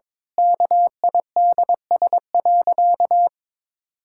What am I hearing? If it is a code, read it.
Morse code, heard as a keyed tone, about 22 wpm, 700 Hz. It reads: KIDS.
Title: Pangram Challenge v2 - 22wpm